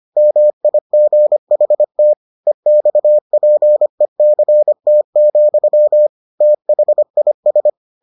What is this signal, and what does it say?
Morse code, heard as a keyed tone, about 25 wpm, 600 Hz. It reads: MIGHT EXPECT, THIS